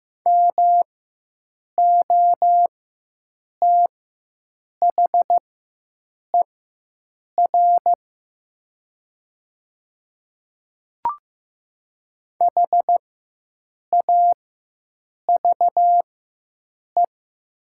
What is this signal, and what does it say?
Morse code, heard as a keyed tone, about 15 wpm, spaced as for 7 wpm, 700 Hz. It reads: MOTHER HAVE
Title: 4x Character Spacing - Top 300 Words - 15wpm